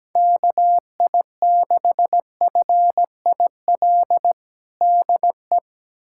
Morse code, heard as a keyed tone, about 17 wpm, 700 Hz. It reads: KI6FIL DE
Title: USA Callsign DE Callsign 17wpm